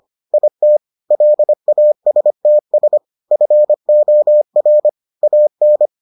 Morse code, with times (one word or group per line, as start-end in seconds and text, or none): 0.3-0.8 IT
1.1-3.0 LASTS
3.3-4.9 FOR
5.2-5.9 AN